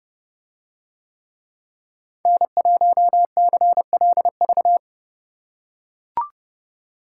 Morse code, heard as a keyed tone, about 30 wpm, 700 Hz. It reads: N1CLV